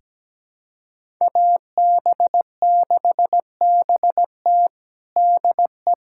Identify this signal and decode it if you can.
Morse code, heard as a keyed tone, about 17 wpm, 700 Hz. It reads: AB6BT DE
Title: USA Callsign DE Callsign 17wpm